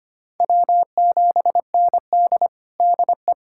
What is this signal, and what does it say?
Morse code, heard as a keyed tone, about 25 wpm, 700 Hz. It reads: W7ND DE
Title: USA Callsign DE Callsign - 4 Characters - 25wpm